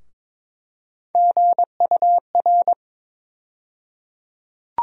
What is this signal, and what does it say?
Morse code, heard as a keyed tone, about 22 wpm, 700 Hz. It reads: GUR
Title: USA Call Sign Postfixes - 3 Letters 22wpm